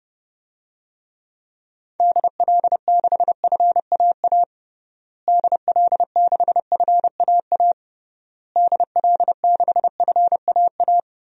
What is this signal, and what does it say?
Morse code, heard as a keyed tone, about 32 wpm, 700 Hz. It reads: DL6FAA DL6FAA DL6FAA
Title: DXCC-100 Call Signs - 6 Characters - Repeated 3x - 30wpm